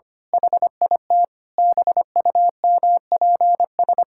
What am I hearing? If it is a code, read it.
Morse code, heard as a keyed tone, about 25 wpm, 700 Hz. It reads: HIT BUMPS
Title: Pangram Challenge v2 - 25wpm